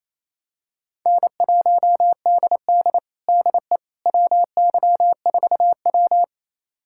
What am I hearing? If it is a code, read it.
Morse code, heard as a keyed tone, about 28 wpm, 700 Hz. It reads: N1DD DE WY4W
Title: USA Callsign DE Callsign - 4 Characters - 28wpm